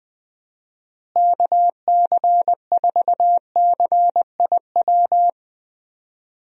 Morse code, heard as a keyed tone, about 21 wpm, 700 Hz. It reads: KC4CIW
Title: USA Call Signs - 6 Characters - Repeated 2x - 20wpm